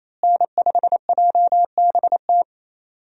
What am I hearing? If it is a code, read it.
Morse code, heard as a keyed tone, about 28 wpm, 700 Hz. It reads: N5JBT
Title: USA Call Signs - 5 Characters - Repeated 3x - 28wpm